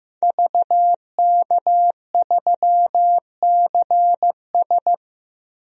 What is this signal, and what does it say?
Morse code, heard as a keyed tone, about 15 wpm, 700 Hz. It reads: VK3CS
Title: DXCC-100 Call Signs - 5 Characters - Repeated 2x - 15wpm